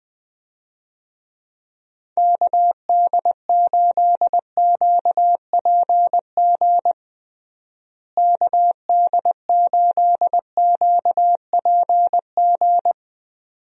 Morse code, heard as a keyed tone, about 20 wpm, 700 Hz. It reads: KD8QPG KD8QPG
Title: USA Call Signs - 6 Characters - Repeated 3x - 20wpm